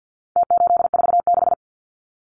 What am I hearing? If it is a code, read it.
Morse code, heard as a keyed tone, about 48 wpm, 700 Hz. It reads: T846
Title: Numbers - Contest Style - 4-Digit 50wpm